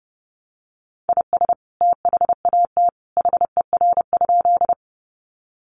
Morse code, heard as a keyed tone, about 30 wpm, 700 Hz. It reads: IS THAT HER?
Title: Sentences from Top 100 Words 30wpm